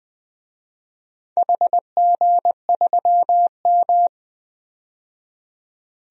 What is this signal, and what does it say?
Morse code, heard as a keyed tone, about 20 wpm, 700 Hz. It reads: HG3M
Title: DXCC-100 Call Signs - 4 Characters - 20wpm